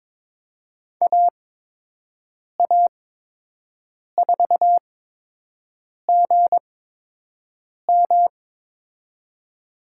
Morse code, heard as a keyed tone, about 22 wpm, spaced as for 6 wpm, 700 Hz. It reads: AA4GM
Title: USA Call Signs - 5 Characters - 8x Character Spacing - 22wpm